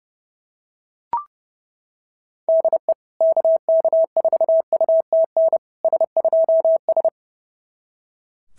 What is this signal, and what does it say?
Morse code, heard as a keyed tone, about 30 wpm, 690 Hz. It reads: DE KK4UTN S2S
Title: SOTA QSO Practice 30wpm